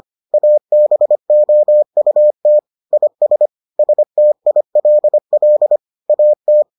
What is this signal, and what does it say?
Morse code, heard as a keyed tone, about 25 wpm, 600 Hz. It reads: ABOUT IS STILL AT